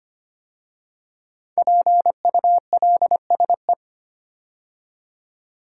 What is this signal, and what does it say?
Morse code, heard as a keyed tone, about 25 wpm, 700 Hz. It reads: PULSE
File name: 5 Letter Words - First 3 Letters Given - 25wpm